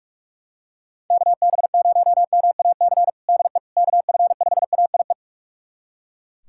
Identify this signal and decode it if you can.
Morse code, heard as a keyed tone, about 41 wpm, 700 Hz. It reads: KB0MAC DE KF5AIE
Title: USA Callsign DE Callsign - 6 Characters - 45wpm